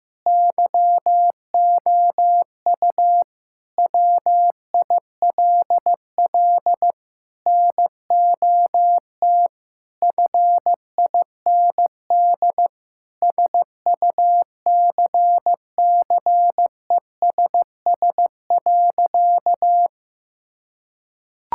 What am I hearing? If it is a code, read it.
Morse code, heard as a keyed tone, about 15 wpm, 700 Hz. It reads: YOU WILL NOT FIND SUCCESS.
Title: Sentences from Top 1000 Words - Review Part 5 - 15wpm